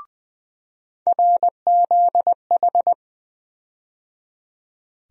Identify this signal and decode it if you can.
Morse code, heard as a keyed tone, about 21 wpm, 700 Hz. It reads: RZH
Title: USA Call Sign Postfixes - 3 Letters 20wpm